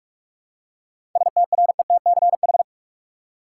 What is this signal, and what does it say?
Morse code, heard as a keyed tone, about 47 wpm, 700 Hz. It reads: STRETCH
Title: Seven Letter Words - 45wpm